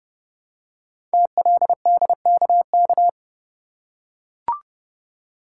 Morse code, heard as a keyed tone, about 28 wpm, 700 Hz. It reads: TLDKK
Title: Code Groups - 5 Letters - 30wpm